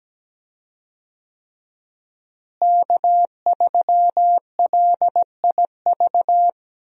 Morse code, heard as a keyed tone, about 17 wpm, 700 Hz. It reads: K3LIV
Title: USA Call Signs - 5 Characters - Repeated 3x - 17wpm